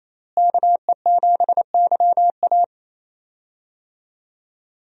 Morse code, heard as a keyed tone, about 28 wpm, 700 Hz. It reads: KE7YA
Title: USA Call Signs - 5 Characters - Repeated 3x - 28wpm